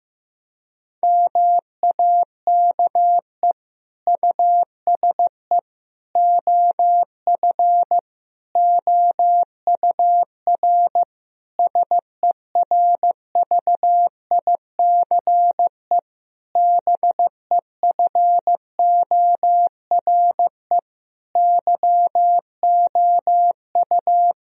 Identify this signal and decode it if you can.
Morse code, heard as a keyed tone, about 15 wpm, 700 Hz. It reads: MAKE USE OF OUR SERVICE BEFORE YOU
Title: Sentences from Top 1000 Words - Review Part 4 - 15wpm